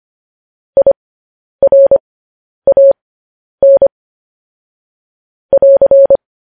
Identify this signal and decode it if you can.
Morse code, heard as a keyed tone, about 25 wpm, spaced as for 10 wpm, 550 Hz. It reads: IRAN <AR>